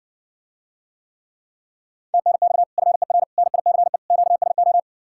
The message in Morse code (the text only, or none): TNX FER NICE QSO